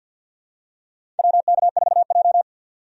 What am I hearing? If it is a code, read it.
Morse code, heard as a keyed tone, about 52 wpm, 700 Hz. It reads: WK4J
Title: USA Call Signs - 4 Characters - 50wpm